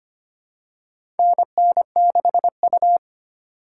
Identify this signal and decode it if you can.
Morse code, heard as a keyed tone, about 25 wpm, 700 Hz. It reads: NN6U
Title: USA Call Signs - 4 Characters - 2x Character Spacing - 25wpm